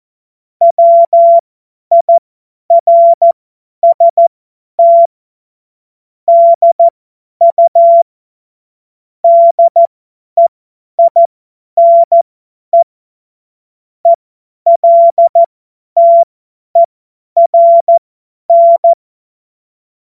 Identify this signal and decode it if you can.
Morse code, heard as a keyed tone, about 14 wpm, spaced as for 10 wpm, 680 Hz. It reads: WIRST DU DEINE ELTERN
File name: Morsekurs / Witze / Witz 02. November 2021, 50 F70